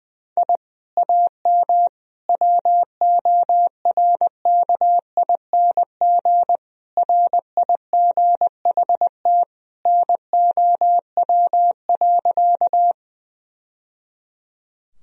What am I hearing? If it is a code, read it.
Morse code, heard as a keyed tone, about 20 wpm, 700 Hz. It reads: I AM WORKING RIGHT NOW.
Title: Sentences from Top 1000 Words - Review Part 4 - 20wpm